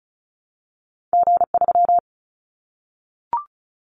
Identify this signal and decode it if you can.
Morse code, heard as a keyed tone, about 34 wpm, 700 Hz. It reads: G3
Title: Callsign Prefixes for Top 25 DXCC - Repeated 3x - 35wpm